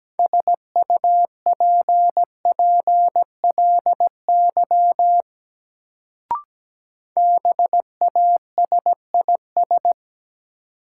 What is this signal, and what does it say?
Morse code, heard as a keyed tone, about 17 wpm, 700 Hz. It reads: SUPPLY BASIS
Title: Sentences from Top 1200 Words - New Word Drill Alt - 17wpm